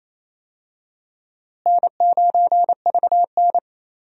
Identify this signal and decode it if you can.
Morse code, heard as a keyed tone, about 28 wpm, 700 Hz. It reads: N9VN
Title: USA Call Signs - 4 Characters - Repeated 3x - 28wpm